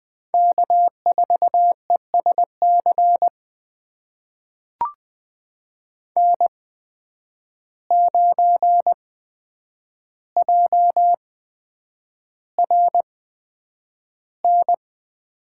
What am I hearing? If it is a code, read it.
Morse code, heard as a keyed tone, about 20 wpm, spaced as for 9 wpm, 700 Hz. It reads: K4ESC N9JRN
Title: USA Call Signs - 5 Characters - 8x Character Spacing - 20wpm